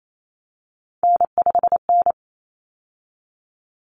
Morse code, heard as a keyed tone, about 28 wpm, 700 Hz. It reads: N5N
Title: Common Cut Numbers - 3-Digit 28wpm